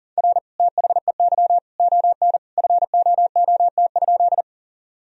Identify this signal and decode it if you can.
Morse code, heard as a keyed tone, about 37 wpm, 700 Hz. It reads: R THEY ON FOOT?
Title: Sentences from Top 500 Words and Abbreviations 40wpm